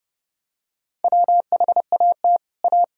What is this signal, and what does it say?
Morse code, heard as a keyed tone, about 30 wpm, 700 Hz. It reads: WHAT A
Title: Sentences from Top 1000 Words - Review Part 2 - 30wpm